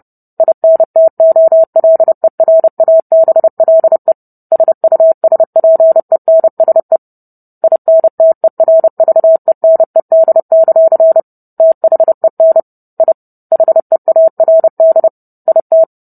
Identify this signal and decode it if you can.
Morse code, heard as a keyed tone, about 30 wpm, 650 Hz. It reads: INTOLERABLE SUSPENSE INTERVENED; THEN I HEARD IT